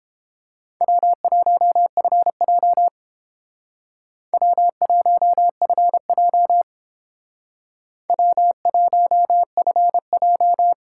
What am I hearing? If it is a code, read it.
Morse code, heard as a keyed tone, about 29 wpm, 700 Hz. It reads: W1FJ W1FJ W1FJ